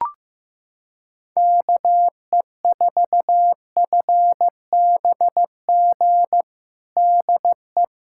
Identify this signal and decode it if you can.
Morse code, heard as a keyed tone, about 15 wpm, 700 Hz. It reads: KE4FBG DE
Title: USA Callsign DE Callsign - 6 Characters - 15wpm